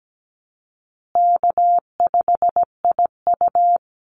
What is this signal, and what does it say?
Morse code, heard as a keyed tone, about 17 wpm, 700 Hz. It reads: K5IU
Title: Easy Call Signs Repeated 3x - 17wpm